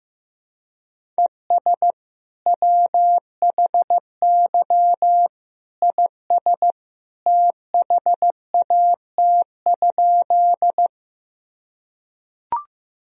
Morse code, heard as a keyed tone, about 15 wpm, 700 Hz. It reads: ES WHY IS THAT?
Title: Sentences from Top 200 Words and Abbreviations 15wpm